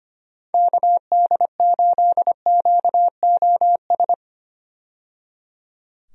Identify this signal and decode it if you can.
Morse code, heard as a keyed tone, about 25 wpm, 700 Hz. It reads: KD8QOS
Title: USA Call Signs - 6 Characters - 25wpm